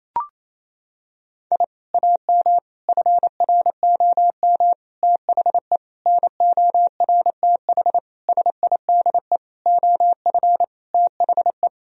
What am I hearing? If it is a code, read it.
Morse code, heard as a keyed tone, about 28 wpm, 700 Hz. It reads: I AM FROM THE NORTH SIDE OF THE
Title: Sentences from Top 1000 Words - Review Part 1 - 28wpm